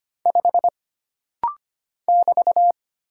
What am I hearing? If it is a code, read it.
Morse code, heard as a keyed tone, about 25 wpm, 700 Hz. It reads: 5 <BT>
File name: Single Character (Letter:Number:Punctuation:Prosign) - Rapid-Fire - 25wpm